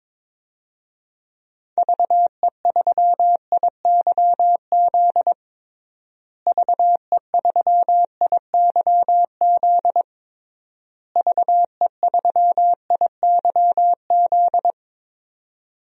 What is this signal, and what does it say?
Morse code, heard as a keyed tone, about 22 wpm, 700 Hz. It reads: VE3IYZ VE3IYZ VE3IYZ
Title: DXCC-100 Call Signs - 6 Characters - Repeated 3x - 22wpm